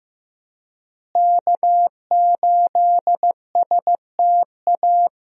Morse code, heard as a keyed tone, about 15 wpm, 700 Hz. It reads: K8STA
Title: USA Call Signs - 5 Characters - Repeated 2x - 15wpm